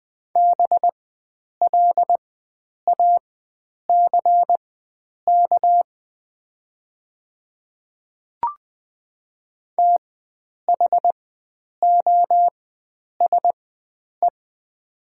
Morse code, heard as a keyed tone, about 20 wpm, spaced as for 9 wpm, 700 Hz. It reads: BLACK THOSE
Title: 4x Character Spacing - Top 500 Words - 20wpm